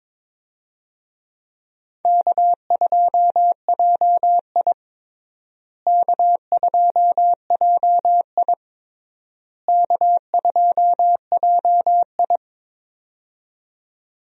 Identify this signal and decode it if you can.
Morse code, heard as a keyed tone, about 22 wpm, 700 Hz. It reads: K2JI K2JI K2JI
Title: USA Call Signs - 4 Characters - Repeated 3x - 22wpm